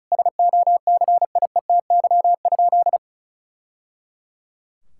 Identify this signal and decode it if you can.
Morse code, heard as a keyed tone, about 34 wpm, 700 Hz. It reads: SOCIETY?
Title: Sentences from Top 1000 Words - Review Part 6 - 35wpm